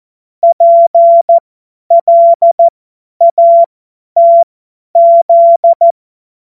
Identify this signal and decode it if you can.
Morse code, heard as a keyed tone, about 14 wpm, spaced as for 10 wpm, 680 Hz. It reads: PLATZ
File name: Morsekurs / Witze / Witz 15. November 2022 50 F70